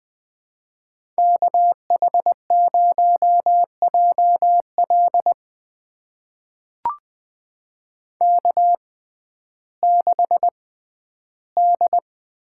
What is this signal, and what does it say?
Morse code, heard as a keyed tone, about 20 wpm, spaced as for 12 wpm, 700 Hz. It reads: KH0JL K6D
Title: USA Call Signs - 5 Characters - 6x Character Spacing - 20wpm